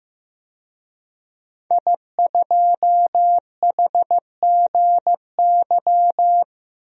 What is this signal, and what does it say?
Morse code, heard as a keyed tone, about 15 wpm, 700 Hz. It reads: I2HGY
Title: DXCC-100 Call Signs - 5 Characters - 15wpm